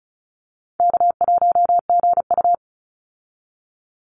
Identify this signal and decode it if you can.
Morse code, heard as a keyed tone, about 34 wpm, 700 Hz. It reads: K1GU
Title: Call Signs Repeated 2x - 35wpm